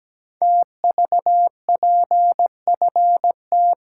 Morse code, heard as a keyed tone, about 17 wpm, 700 Hz. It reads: TVPFT
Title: Code Groups - 5 Letters - 17wpm